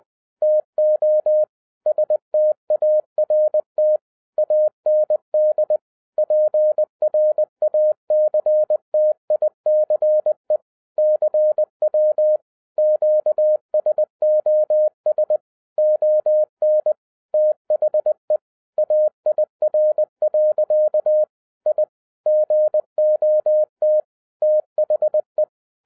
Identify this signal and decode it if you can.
Morse code, heard as a keyed tone, about 20 wpm, 610 Hz. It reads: TO START AND PRACTICE CW QSOS ON THE AIR. I GOT THE